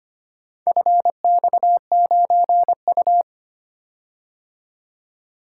Morse code, heard as a keyed tone, about 25 wpm, 700 Hz. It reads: FX9U
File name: practice / 25wpm / Four Letter-Number Combinations - 25wpm